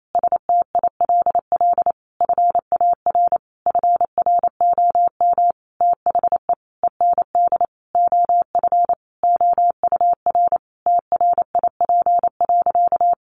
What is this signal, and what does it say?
Morse code, heard as a keyed tone, about 28 wpm, 700 Hz. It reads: STILL FAR FROM THE END OF OUR TRIP.
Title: Sentences from Top 1300 Words 28wpm